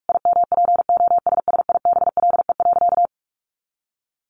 Morse code, heard as a keyed tone, about 44 wpm, 700 Hz. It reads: IMPOSSIBLE.
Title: Sentences from Top 1100 Words 45wpm